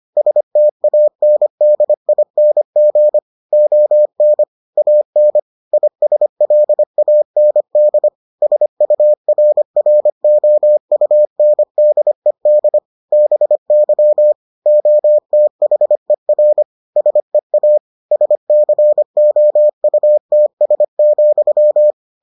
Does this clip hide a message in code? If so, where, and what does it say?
Yes: Morse code from 0.2 to 21.9 s, heard as a keyed tone, about 25 wpm, 600 Hz. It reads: STANDING ON AN ISLAND SURROUNDED BY OTHER SEA SCOUTS,